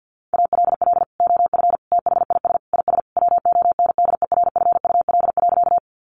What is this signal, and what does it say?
Morse code, heard as a keyed tone, about 50 wpm, 700 Hz. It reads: ALL OF THIS IS WONDERFUL.
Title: Sentences from Top 1100 Words 50wpm